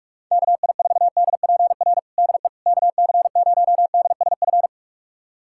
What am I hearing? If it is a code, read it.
Morse code, heard as a keyed tone, about 46 wpm, 700 Hz. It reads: KI4DPR DE KC0DSF
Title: USA Callsign DE Callsign 45wpm